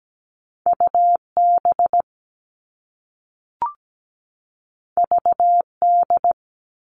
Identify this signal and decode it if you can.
Morse code, heard as a keyed tone, about 17 wpm, 700 Hz. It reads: UB VD
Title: All Cut Numbers - 2-Digit 17wpm